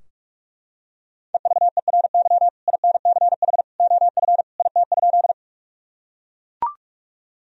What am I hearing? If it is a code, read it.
Morse code, heard as a keyed tone, about 41 wpm, 700 Hz. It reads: EVERY INCH OF IT?
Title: Sentences from Top 1000 Words - Review Part 3 - 45wpm